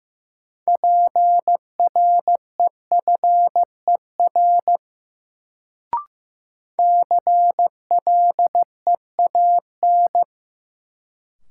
Morse code, heard as a keyed tone, about 15 wpm, 700 Hz. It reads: PREFER CLEAN
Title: Sentences from Top 1100 Words - New Word Drill - 15wpm